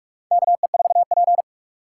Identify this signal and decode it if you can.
Morse code, heard as a keyed tone, about 46 wpm, 700 Hz. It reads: KE4P
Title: USA Call Signs - 4 Characters - 4x Character Spacing - 45wpm